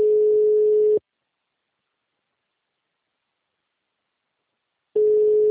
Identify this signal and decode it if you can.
Ringback tone (Europe, 425 Hz).
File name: out- -2013- - - .318